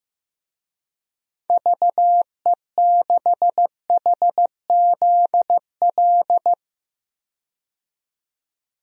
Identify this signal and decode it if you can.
Morse code, heard as a keyed tone, about 15 wpm, 700 Hz. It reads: VE6HZL
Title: DXCC-100 Call Signs - 6 Characters - 15wpm